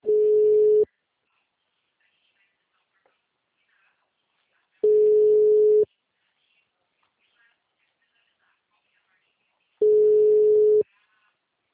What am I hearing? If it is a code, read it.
Ringback tone (Europe, 425 Hz).
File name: out- -2003- - - .74126